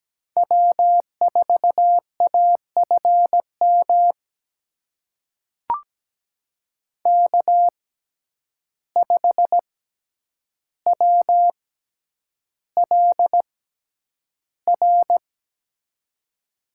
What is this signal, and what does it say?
Morse code, heard as a keyed tone, about 17 wpm, spaced as for 9 wpm, 700 Hz. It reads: W4AFM K5WLR
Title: USA Call Signs - 5 Characters - 6x Character Spacing - 17wpm